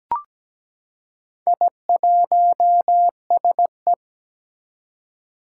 Morse code, heard as a keyed tone, about 17 wpm, 700 Hz. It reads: I1SE
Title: Four Challenging Letter-Number Combinations 17wpm